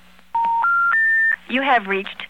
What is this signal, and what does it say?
Special information tone, with speech.
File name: mp3files / off the hook -128